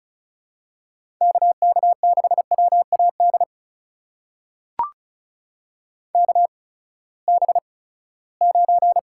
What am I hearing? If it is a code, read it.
Morse code, heard as a keyed tone, about 36 wpm, spaced as for 20 wpm, 700 Hz. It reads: KK6WAD KB9
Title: USA Call Signs - 8x Character Spacing - 35wpm